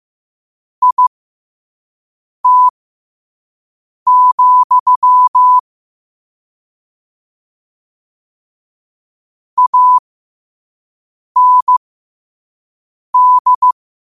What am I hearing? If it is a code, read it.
Morse code, heard as a keyed tone, about 15 wpm, spaced as for 5 wpm, 1000 Hz. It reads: IT, AND
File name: qotdfiles / 5wpm